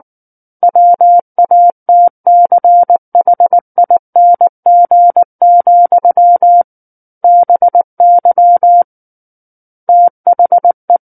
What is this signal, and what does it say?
Morse code, heard as a keyed tone, about 19 wpm, 700 Hz. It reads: WATCHING, BY THE
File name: Book One / 10-Chapter 10 - 19 WPM